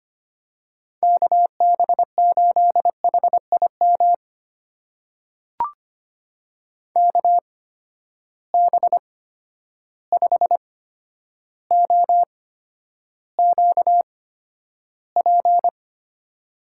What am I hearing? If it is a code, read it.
Morse code, heard as a keyed tone, about 25 wpm, spaced as for 11 wpm, 700 Hz. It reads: KB8HIM KB5OQP
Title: USA Call Signs - 6 Characters - 8x Character Spacing - 25wpm